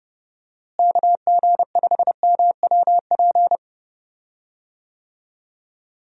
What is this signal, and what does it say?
Morse code, heard as a keyed tone, about 32 wpm, 700 Hz. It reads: KG5MWP
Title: USA Call Signs - 6 Characters - Repeated 3x - 30wpm